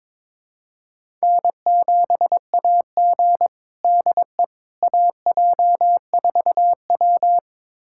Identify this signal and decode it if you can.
Morse code, heard as a keyed tone, about 22 wpm, 700 Hz. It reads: N7AG DE AJ4W
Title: USA Callsign DE Callsign - 4 Characters - 22wpm